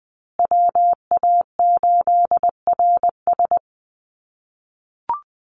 Morse code, heard as a keyed tone, about 20 wpm, 700 Hz. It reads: WA8RS
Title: USA Call Signs Repeated 3x 20wpm